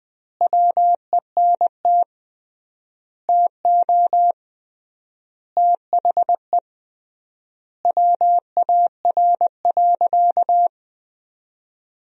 Morse code, heard as a keyed tone, about 20 wpm, 700 Hz. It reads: WENT TO THE WAR.
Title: 3x Word-Spacing - Sentences from Top 400 Words - 20wpm